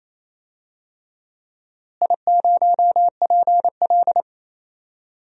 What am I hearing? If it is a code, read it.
Morse code, heard as a keyed tone, about 27 wpm, 700 Hz. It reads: I0PL
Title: DXCC-100 Call Signs - 4 Characters - Repeated 3x - 28wpm